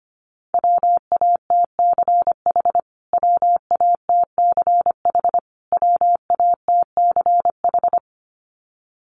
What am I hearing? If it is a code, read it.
Morse code, heard as a keyed tone, about 25 wpm, 700 Hz. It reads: WATCH WATCH WATCH
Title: Top 500 Words Repeated 25wpm